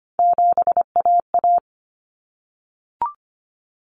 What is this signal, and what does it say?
Morse code, heard as a keyed tone, about 25 wpm, 700 Hz. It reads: 7AA
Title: Common Cut Numbers - 3-Digit 25wpm